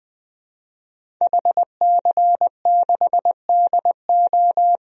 Morse code, heard as a keyed tone, about 20 wpm, 700 Hz. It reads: HC6DO
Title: DXCC-100 Call Signs - 5 Characters - Repeated 3x - 20wpm